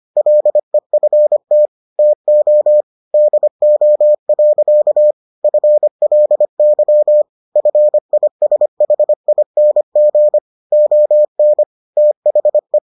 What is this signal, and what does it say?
Morse code, heard as a keyed tone, about 25 wpm, 600 Hz. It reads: LEFT TO DO. FLY FISHING ON THE